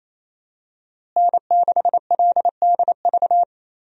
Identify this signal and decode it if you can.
Morse code, heard as a keyed tone, about 27 wpm, 700 Hz. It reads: N6LDV
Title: USA Call Signs - 5 Characters - Repeated 2x - 28wpm